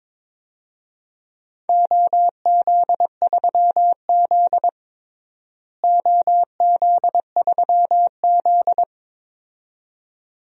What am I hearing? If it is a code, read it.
Morse code, heard as a keyed tone, about 22 wpm, 700 Hz. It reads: OZ3Z OZ3Z
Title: DXCC-100 Call Signs - 4 Characters - Repeated 2x - 22wpm